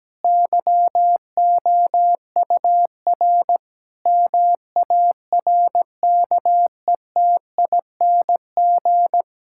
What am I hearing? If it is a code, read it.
Morse code, heard as a keyed tone, about 17 wpm, 700 Hz. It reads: YOUR MARKETING
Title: Sentences from Top 1000 Words - Review Part 5 - 17wpm